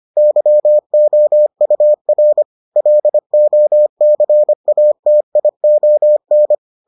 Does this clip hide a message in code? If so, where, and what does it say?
Yes: Morse code from 0.2 to 6.5 s, heard as a keyed tone, about 25 wpm, 600 Hz. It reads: YOUR LOCATION